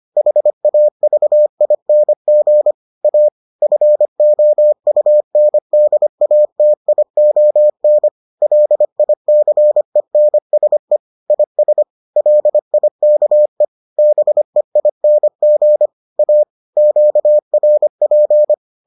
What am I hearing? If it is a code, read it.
Morse code, heard as a keyed tone, about 25 wpm, 600 Hz. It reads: HAVING A FOUNDATION LICENSE IS LIKE BEING A QRP